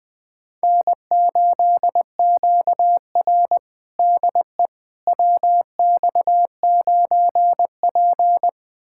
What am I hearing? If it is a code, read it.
Morse code, heard as a keyed tone, about 20 wpm, 700 Hz. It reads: N8QR DE WX9P
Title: USA Callsign DE Callsign - 4 Characters - 20wpm